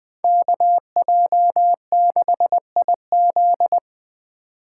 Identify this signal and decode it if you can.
Morse code, heard as a keyed tone, about 20 wpm, 700 Hz. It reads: KJ6IZ
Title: USA Call Signs - 2x Character Spacing - 20wpm